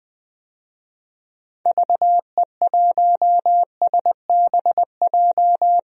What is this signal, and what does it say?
Morse code, heard as a keyed tone, about 20 wpm, 700 Hz. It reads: VE1SBJ
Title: DXCC-100 Call Signs - 6 Characters - 20wpm